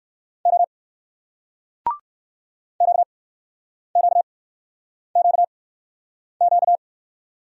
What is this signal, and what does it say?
Morse code, heard as a keyed tone, about 60 wpm, spaced as for 9 wpm, 700 Hz. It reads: O QQQQ